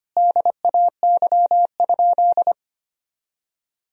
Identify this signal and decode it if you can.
Morse code, heard as a keyed tone, about 25 wpm, 700 Hz. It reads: DAY?
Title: Sentences from Top 1000 Words - Review Part 2 - 25wpm